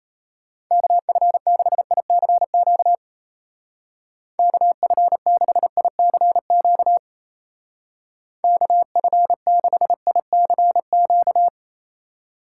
Morse code, heard as a keyed tone, about 35 wpm, 700 Hz. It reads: KF6ICQ KF6ICQ KF6ICQ